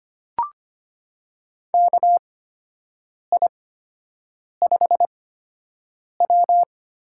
Morse code, heard as a keyed tone, about 25 wpm, spaced as for 7 wpm, 700 Hz. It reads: KI5W
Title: USA Call Signs - 4 Characters - 8x Character Spacing - 25wpm